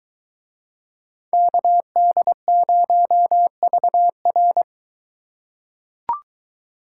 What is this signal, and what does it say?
Morse code, heard as a keyed tone, about 22 wpm, 700 Hz. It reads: KD0VR